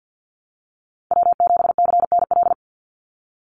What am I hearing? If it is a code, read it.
Morse code, heard as a keyed tone, about 48 wpm, 700 Hz. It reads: W7CNL
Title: NA SOTA Chaser Callsigns 50wpm